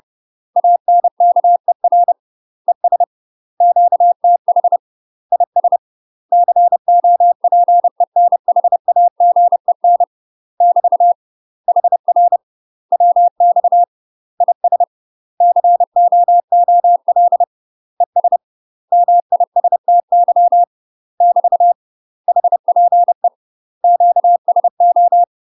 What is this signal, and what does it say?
Morse code, heard as a keyed tone, about 30 wpm, 700 Hz. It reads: ANKER ES QTH IS COPENHAGEN <BT> HR WX IS COOL ES MISTY <BT> HPE QSO
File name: UK MASTER QSOs TRAINING / 30WPM / UK MASTER QSOs 0029